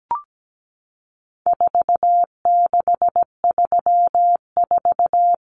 Morse code, heard as a keyed tone, about 17 wpm, 700 Hz. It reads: 4634
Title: Numbers - 4-Digit 17wpm